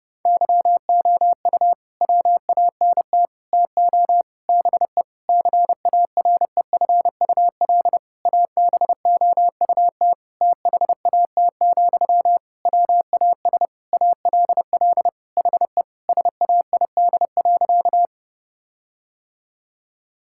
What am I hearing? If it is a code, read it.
Morse code, heard as a keyed tone, about 30 wpm, 700 Hz. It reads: YOU WANT TO BE CAREFUL ABOUT THAT, WAS ALL HE SAID.
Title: Sentences from Top 1600 Words 30wpm